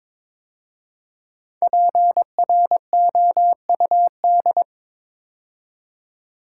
Morse code, heard as a keyed tone, about 22 wpm, 700 Hz. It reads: PROUD
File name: 5 Letter Words - First 3 Letters Given - 22wpm